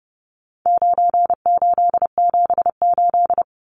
Morse code, heard as a keyed tone, about 30 wpm, 700 Hz. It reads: 9878
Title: Numbers - 4-Digit 30wpm